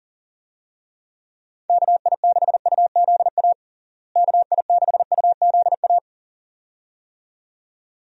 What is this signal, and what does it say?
Morse code, heard as a keyed tone, about 40 wpm, 700 Hz. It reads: KI6UZA KI6UZA
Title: USA Call Signs - 6 Characters - Repeated 2x - 40wpm